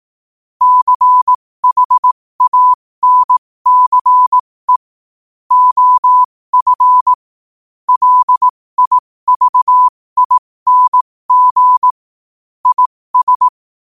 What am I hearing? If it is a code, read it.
Morse code, heard as a keyed tone, about 18 wpm, spaced as for 15 wpm, 1000 Hz. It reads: CHANCE OF LIVING IS
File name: qotdfiles / 15wpm